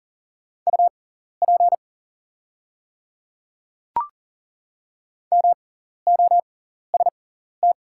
Morse code, heard as a keyed tone, about 40 wpm, spaced as for 14 wpm, 700 Hz. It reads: UP MOST
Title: 6x Character Spacing - Top 100 Words - 40wpm